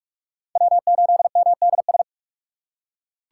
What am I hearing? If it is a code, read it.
Morse code, heard as a keyed tone, about 46 wpm, 700 Hz. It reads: W8MDS
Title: USA Call Signs - 5 Characters - 3x Character Spacing - 45wpm